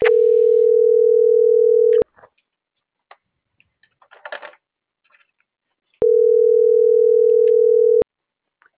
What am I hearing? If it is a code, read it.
Ringback tone (North America), 440 + 480 Hz.
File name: CENTRAL MABE / out karen.maya